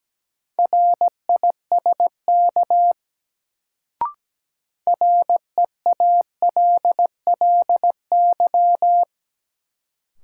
Morse code, heard as a keyed tone, about 17 wpm, 700 Hz. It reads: RISK REALLY
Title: Sentences from Top 600 Words - New Word Drill - 17wpm